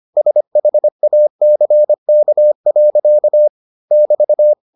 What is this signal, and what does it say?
Morse code, heard as a keyed tone, about 25 wpm, 600 Hz. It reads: SHACK. <BT>